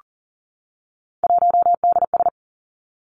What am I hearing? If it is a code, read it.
Morse code, heard as a keyed tone, about 43 wpm, 700 Hz. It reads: 1DS
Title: Three Challenging Letter-Number Combinations 40wpm